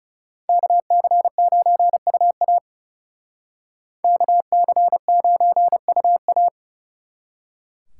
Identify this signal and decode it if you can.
Morse code, heard as a keyed tone, about 38 wpm, 700 Hz. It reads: KC9UA KC9UA